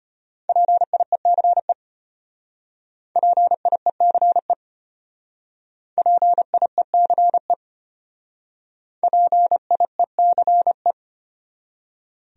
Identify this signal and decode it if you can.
Morse code, heard as a keyed tone, about 30 wpm, 700 Hz. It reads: PIECE PIECE PIECE PIECE
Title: Speed-Racing Top 500 Words 25wpm